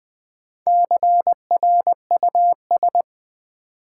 Morse code, heard as a keyed tone, about 20 wpm, 700 Hz. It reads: CRUS
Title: Build A Word - 1 Letter At A Time - 3 to 5-Letter Words - 20wpm